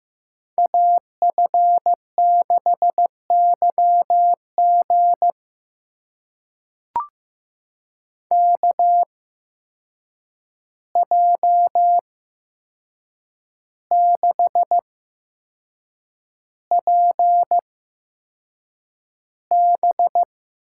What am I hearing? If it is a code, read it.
Morse code, heard as a keyed tone, about 15 wpm, spaced as for 6 wpm, 700 Hz. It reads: AF6YG KJ6PB
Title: USA Call Signs - 8x Character Spacing - 15wpm